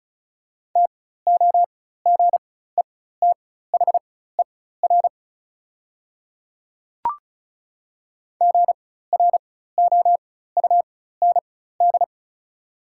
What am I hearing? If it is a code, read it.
Morse code, heard as a keyed tone, about 36 wpm, spaced as for 16 wpm, 690 Hz. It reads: TOGETHER GROUND
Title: 4x Character Spacing - Top 500 Words - 35wpm